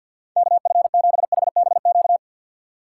Morse code, heard as a keyed tone, about 64 wpm, 700 Hz. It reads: KF7HBQ